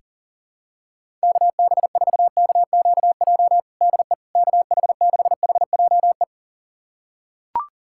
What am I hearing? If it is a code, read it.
Morse code, heard as a keyed tone, about 40 wpm, 700 Hz. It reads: KB4KQJ DE KH6HJE
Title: USA Callsign DE Callsign 40wpm